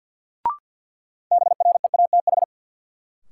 Morse code, heard as a keyed tone, about 52 wpm, 700 Hz. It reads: BREATH
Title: Sets of 5 Words Encore - New Word Drill - 50wpm